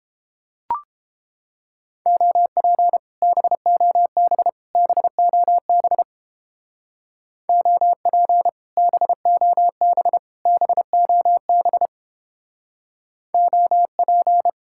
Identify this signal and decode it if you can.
Morse code, heard as a keyed tone, about 29 wpm, 700 Hz. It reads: OP BOB BOB OP BOB BOB OP